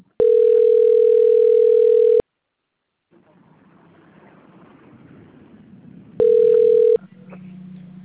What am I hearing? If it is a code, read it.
Ringback tone (North America), 440 + 480 Hz.
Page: Ribeirao Preto